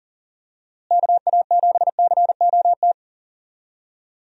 Morse code, heard as a keyed tone, about 39 wpm, 700 Hz. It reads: KA7COT
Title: USA Call Signs - 6 Characters - Repeated 2x - 40wpm